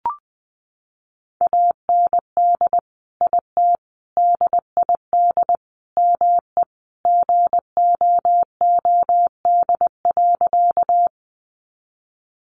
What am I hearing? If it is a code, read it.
Morse code, heard as a keyed tone, about 20 wpm, 700 Hz. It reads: AND IT DID ME GOOD.
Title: Sentences from Top 200 Words 20wpm